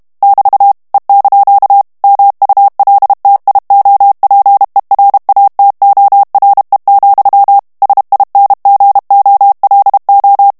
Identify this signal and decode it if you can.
Morse code, heard as a keyed tone, about 32 wpm, 800 Hz. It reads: <BT> E) MULTIOPERATORE, SINGOLO
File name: texte clair 2 vitesse 30 mots minute